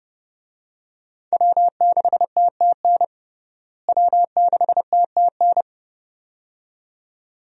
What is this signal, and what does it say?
Morse code, heard as a keyed tone, about 31 wpm, 700 Hz. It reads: W6TTN W6TTN
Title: USA Call Signs - 5 Characters - Repeated 2x - 30wpm